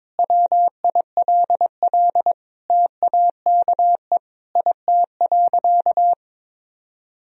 Morse code, heard as a keyed tone, about 23 wpm, 700 Hz. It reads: WILL TAKE IT.
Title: Sentences from Top 1000 Words - Review Part 3 - 22wpm